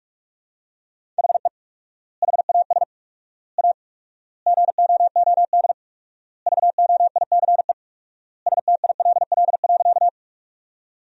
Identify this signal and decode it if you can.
Morse code, heard as a keyed tone, about 42 wpm, 690 Hz. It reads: HE HAS A GOOD VOICE STILL.
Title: 4x Word-Spacing - Sentences from Top 500 Words - 45wpm